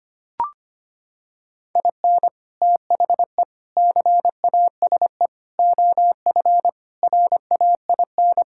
Morse code, heard as a keyed tone, about 25 wpm, 700 Hz. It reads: IN THE CASE OF RAIN
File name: practice / 25wpm / Sentences with 4-Word Phrases - 25wpm